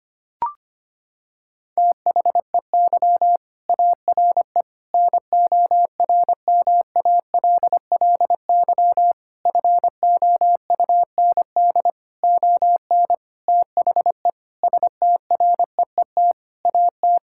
Morse code, heard as a keyed tone, about 25 wpm, 700 Hz. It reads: THEY ARE NORMALLY FOUND ON THE STREET AT
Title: Sentences from Top 1000 Words - Review Part 6 - 25wpm